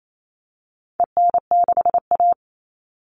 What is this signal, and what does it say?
Morse code, heard as a keyed tone, about 28 wpm, 700 Hz. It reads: EN6A
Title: All Cut Numbers - 4-Digit 28wpm